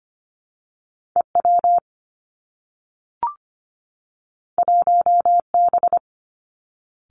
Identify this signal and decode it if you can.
Morse code, heard as a keyed tone, about 25 wpm, 700 Hz. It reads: E W 1 B
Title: Two Challenging Letter-Number Combinations 25wpm